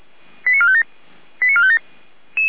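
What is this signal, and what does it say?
Five-tone selective call (ZVEI), heard as quick runs of five tones, tones of about 70 ms. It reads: 89457 89457
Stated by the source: ZVEI five-tone call received by SDR, recorded by fire brigade Feuerwehr Orlamünde.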